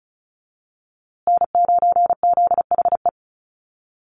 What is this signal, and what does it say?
Morse code, heard as a keyed tone, about 34 wpm, 700 Hz. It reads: N9ZHE
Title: USA Call Signs Repeated 2x 35wpm